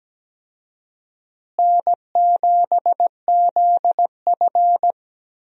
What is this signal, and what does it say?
Morse code, heard as a keyed tone, about 17 wpm, 700 Hz. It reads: N7ZF
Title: USA Call Signs - 4 Characters - Repeated 2x - 17wpm